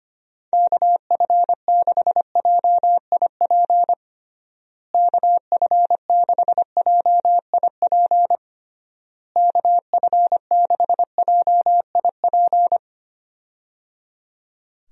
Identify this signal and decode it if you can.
Morse code, heard as a keyed tone, about 25 wpm, 700 Hz. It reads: KF6JIP KF6JIP KF6JIP
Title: USA Call Signs - 6 Characters - Repeated 3x - 25wpm